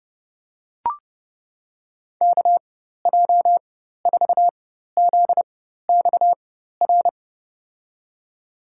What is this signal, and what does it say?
Morse code, heard as a keyed tone, about 32 wpm, spaced as for 14 wpm, 700 Hz. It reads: KJ4ZXR
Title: USA Call Signs - 6 Characters - 4x Character Spacing - 30wpm